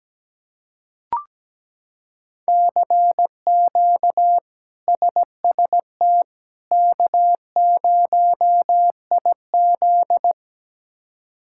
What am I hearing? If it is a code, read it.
Morse code, heard as a keyed tone, about 17 wpm, 700 Hz. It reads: CQ SST K0IZ
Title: K1USN SST QSO Practice - 17wpm